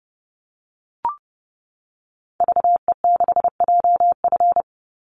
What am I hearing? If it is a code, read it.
Morse code, heard as a keyed tone, about 30 wpm, 700 Hz. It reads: VE6JF
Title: Call Signs - 30wpm